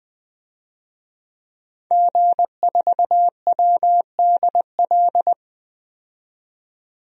Morse code, heard as a keyed tone, about 21 wpm, 700 Hz. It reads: G4WDL
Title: DXCC-100 Call Signs - 5 Characters - 20wpm